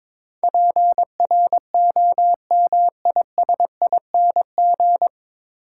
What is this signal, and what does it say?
Morse code, heard as a keyed tone, about 22 wpm, 700 Hz. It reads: PROMISING
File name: 9 Letter Words - First 3 Letters Given - 22wpm